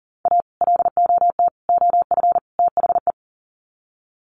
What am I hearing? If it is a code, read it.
Morse code, heard as a keyed tone, about 40 wpm, 700 Hz. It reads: A LOT OF THE
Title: Sets of 4 Words 40wpm